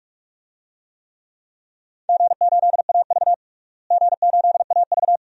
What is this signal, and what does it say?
Morse code, heard as a keyed tone, about 46 wpm, 700 Hz. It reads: G8AV G8AV
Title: DXCC-100 Call Signs - 4 Characters - Repeated 2x - 45wpm